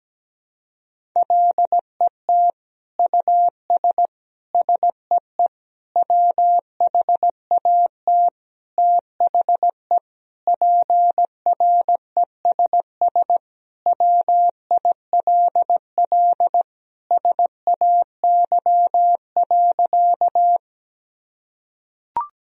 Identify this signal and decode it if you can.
Morse code, heard as a keyed tone, about 17 wpm, 700 Hz. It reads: LET US SEE WHAT THE PRESS WILL SAY.
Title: Sentences from Top 1000 Words - Review Part 1 - 17wpm